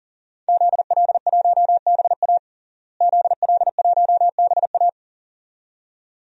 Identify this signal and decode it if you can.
Morse code, heard as a keyed tone, about 40 wpm, 700 Hz. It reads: ZL1BA ZL1BA
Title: DXCC-100 Call Signs - 5 Characters - Repeated 2x - 40wpm